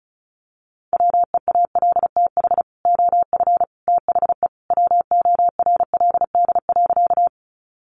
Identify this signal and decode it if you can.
Morse code, heard as a keyed tone, about 34 wpm, 700 Hz. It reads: WEALTH OF THE WORLD.
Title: Sentences from Top 1200 Words 35wpm